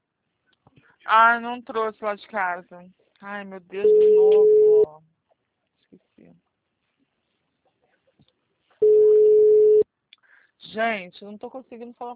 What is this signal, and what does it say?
Ringback tone (Europe, 425 Hz), with speech.
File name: out- -2018- - - .49347